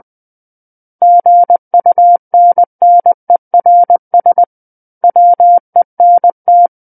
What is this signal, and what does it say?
Morse code, heard as a keyed tone, about 20 wpm, 700 Hz. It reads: GUNNERS WENT
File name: 20 WPM